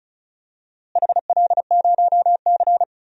Morse code, heard as a keyed tone, about 34 wpm, 700 Hz. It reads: HL0C
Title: DXCC-100 Call Signs - 4 Characters - Repeated 2x - 35wpm